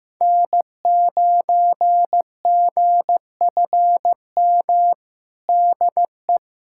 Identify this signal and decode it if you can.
Morse code, heard as a keyed tone, about 15 wpm, 700 Hz. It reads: N9GFM DE
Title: USA Callsign DE Callsign 15wpm